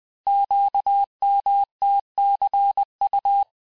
Morse code, heard as a keyed tone, about 20 wpm, 800 Hz. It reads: QMTCU
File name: cours-858-texte-chiffre-vitesse-19mpm